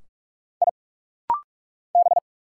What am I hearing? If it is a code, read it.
Morse code, heard as a keyed tone, about 46 wpm, 700 Hz. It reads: I B
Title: Single Letter-Number - Rapid-Fire 45wpm